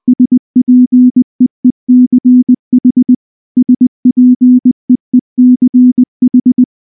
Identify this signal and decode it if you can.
Morse code, heard as a keyed tone, about 20 wpm, 250 Hz. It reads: SPEECH SPEECH